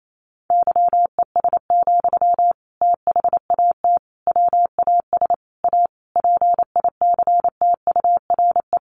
Morse code, heard as a keyed tone, about 28 wpm, 700 Hz. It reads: YES, THAT WAS A PICTURE
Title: Sentences from Top 500 Words 28wpm